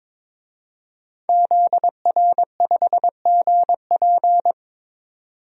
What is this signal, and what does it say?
Morse code, heard as a keyed tone, about 22 wpm, 700 Hz. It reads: ZR5GP
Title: DXCC-100 Call Signs - 5 Characters - 22wpm